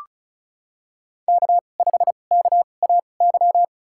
Morse code, heard as a keyed tone, about 34 wpm, spaced as for 25 wpm, 700 Hz. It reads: K5KAY
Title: USA Call Signs - 5 Characters - 2x Character Spacing - 35wpm